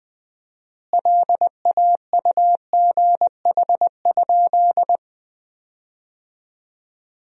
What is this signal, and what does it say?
Morse code, heard as a keyed tone, about 20 wpm, 700 Hz. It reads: LAUGH?
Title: 4x Word-Spacing - Sentences from Top 500 Words - 20wpm